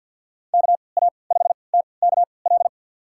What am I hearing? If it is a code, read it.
Morse code, heard as a keyed tone, about 52 wpm, spaced as for 29 wpm, 700 Hz. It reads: KA5TKL
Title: USA Call Signs - 6 Characters - 3x Character Spacing - 50wpm